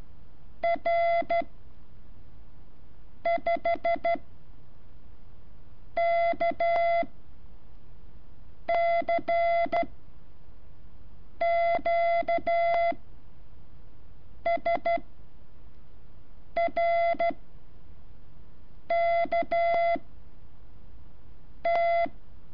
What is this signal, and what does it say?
Morse code, heard as a keyed tone, about 12 wpm, 680 Hz. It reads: R 5 K C Q S R K T
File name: curso / ejercicio 16